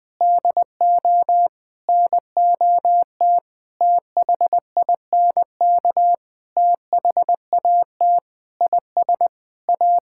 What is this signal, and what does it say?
Morse code, heard as a keyed tone, about 19 wpm, 700 Hz. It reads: DO NOT THINK THAT IS A
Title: Sentences with 5-Word Phrases - Spoken Beginning - 20wpm